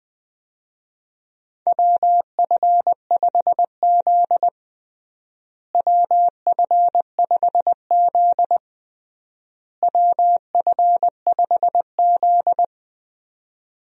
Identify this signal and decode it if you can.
Morse code, heard as a keyed tone, about 20 wpm, 700 Hz. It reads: WF5Z WF5Z WF5Z
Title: USA Call Signs - 4 Characters - Repeated 3x - 20wpm